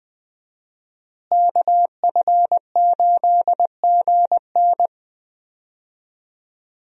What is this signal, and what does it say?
Morse code, heard as a keyed tone, about 20 wpm, 700 Hz. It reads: KF8GN
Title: USA Call Signs - 5 Characters - Repeated 2x - 20wpm